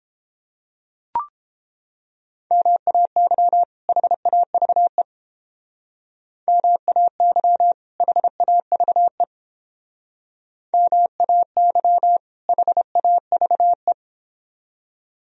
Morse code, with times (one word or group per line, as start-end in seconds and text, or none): 2.5-3.6 MAY
3.9-5.0 HAVE
6.5-7.7 MAY
8.0-9.2 HAVE
10.7-12.2 MAY
12.5-13.9 HAVE